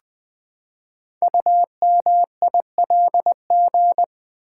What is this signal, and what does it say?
Morse code, heard as a keyed tone, about 20 wpm, 700 Hz. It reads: UMILG
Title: Code Groups - 5 Letters - 20wpm